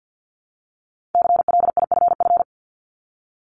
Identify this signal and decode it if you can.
Morse code, heard as a keyed tone, about 50 wpm, 700 Hz. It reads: CLIFF
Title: Multi-letter Phonemes Repeated 3x 50wpm